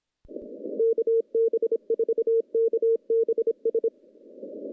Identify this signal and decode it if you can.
Morse code, heard as a keyed tone, about 25 wpm, 460 Hz. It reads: KB4KBS
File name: contests / CW-SS-k5tr- -0200